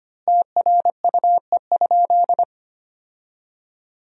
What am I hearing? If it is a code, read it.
Morse code, heard as a keyed tone, about 25 wpm, 700 Hz. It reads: TRUE?
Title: 3x Word-Spacing - Sentences from Top 400 Words - 25wpm